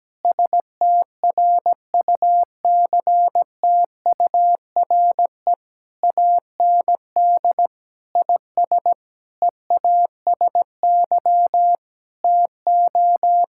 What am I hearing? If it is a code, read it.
Morse code, heard as a keyed tone, about 17 wpm, 700 Hz. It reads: STRUCTURE AND IS EASY TO
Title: Sentences from Top 1000 Words - Review Part 6 - 17wpm